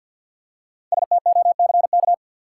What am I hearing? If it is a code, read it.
Morse code, heard as a keyed tone, about 53 wpm, 700 Hz. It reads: STOCK
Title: Five Letter Words - 50wpm